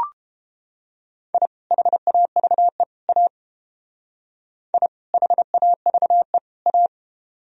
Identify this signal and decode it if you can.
Morse code, heard as a keyed tone, about 30 wpm, 700 Hz. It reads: I HAVE A I HAVE A